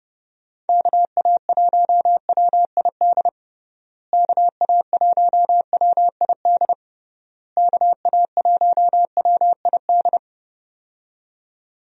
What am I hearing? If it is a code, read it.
Morse code, heard as a keyed tone, about 29 wpm, 700 Hz. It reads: KA1WID KA1WID KA1WID
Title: USA Call Signs - 6 Characters - Repeated 3x - 30wpm